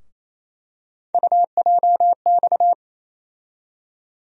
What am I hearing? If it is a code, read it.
Morse code, heard as a keyed tone, about 28 wpm, 700 Hz. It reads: UJX
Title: USA Call Sign Postfixes - 3 Letters 28wpm